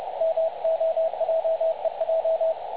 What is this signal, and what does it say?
Morse code, heard as a keyed tone, about 29 wpm, 660 Hz. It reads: MOJEJ
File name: dlazaawansowanych / nagraniazpasma / websdr recording 3565.0kHz